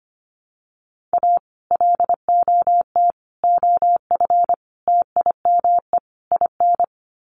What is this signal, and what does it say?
Morse code, heard as a keyed tone, about 25 wpm, 700 Hz. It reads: A LOT OF TIME IN
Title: Sets of 5 Words Encore2 25wpm